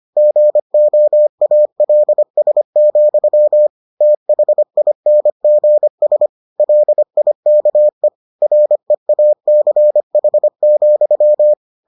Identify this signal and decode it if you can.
Morse code, heard as a keyed tone, about 25 wpm, 600 Hz. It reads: GOALS, THINGS LIKE REACH,